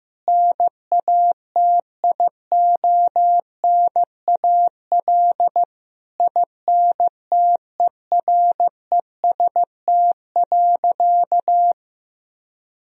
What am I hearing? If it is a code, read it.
Morse code, heard as a keyed tone, about 15 wpm, 700 Hz. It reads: NATIONAL INTEREST.
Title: Sentences from Top 1000 Words - Review Part 4 - 15wpm